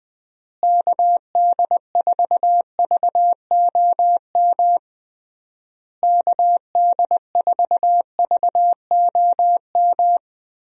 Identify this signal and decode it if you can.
Morse code, heard as a keyed tone, about 20 wpm, 700 Hz. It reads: KD4VOM KD4VOM
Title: USA Call Signs - 6 Characters - Repeated 3x - 20wpm